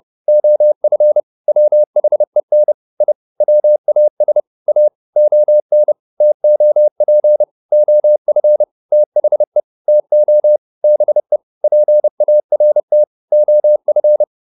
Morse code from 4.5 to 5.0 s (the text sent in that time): A